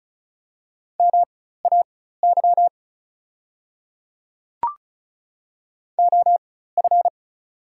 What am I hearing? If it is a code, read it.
Morse code, heard as a keyed tone, about 34 wpm, spaced as for 16 wpm, 700 Hz. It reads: MAY OF
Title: 4x Character Spacing - Top 100 Words - 35wpm